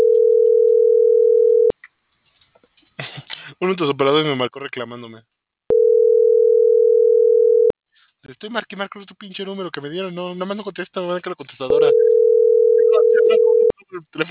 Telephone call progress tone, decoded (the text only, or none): ringback tone